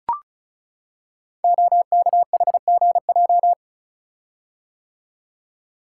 Morse code, heard as a keyed tone, about 36 wpm, 700 Hz. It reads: OKHGJ
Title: Code Groups - 5 Letter:Numbers - 35wpm